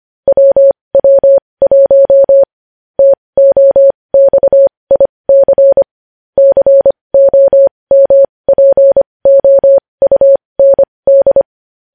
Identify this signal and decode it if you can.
Morse code, heard as a keyed tone, about 25 wpm, spaced as for 20 wpm, 550 Hz. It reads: WW1 TOXIC COMPOUND
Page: News Headlines in Morse Code at 20 WPM